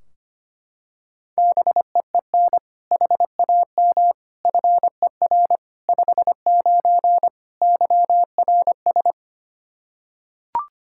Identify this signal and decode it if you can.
Morse code, heard as a keyed tone, about 25 wpm, 700 Hz. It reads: BEEN HAM FER 59 YRS
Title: Practice QSO - Exchange 3 - More Info - 25wpm